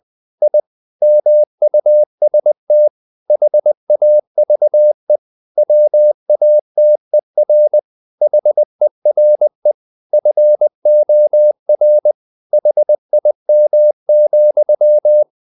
Morse code, heard as a keyed tone, about 20 wpm, 600 Hz. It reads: I MUST HAVE WATER HERE FOR HIM,
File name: the old man and the sea / the old man and the sea part 1 20wpm